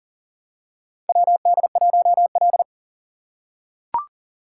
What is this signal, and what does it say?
Morse code, heard as a keyed tone, about 40 wpm, 700 Hz. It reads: WD1L
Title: USA Call Signs - 4 Characters - 2x Character Spacing - 40wpm